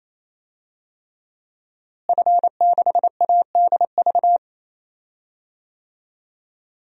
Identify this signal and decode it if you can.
Morse code, heard as a keyed tone, about 28 wpm, 700 Hz. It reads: F6ADV
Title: DXCC-100 Call Signs - 5 Characters - Repeated 3x - 28wpm